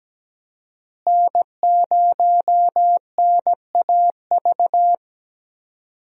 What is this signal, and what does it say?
Morse code, heard as a keyed tone, about 17 wpm, 700 Hz. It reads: N0NAV
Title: USA Call Signs - 5 Characters - Repeated 3x - 17wpm